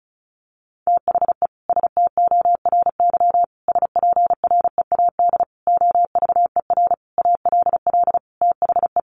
Morse code, heard as a keyed tone, about 34 wpm, 700 Hz. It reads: THE STORY SPREAD OVER ALL THE
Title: Sentences from Top 1200 Words 35wpm